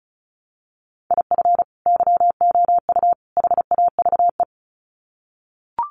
Morse code, heard as a keyed tone, about 34 wpm, 700 Hz. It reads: IF YOU HAVE
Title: Sets of 3 Words 35wpm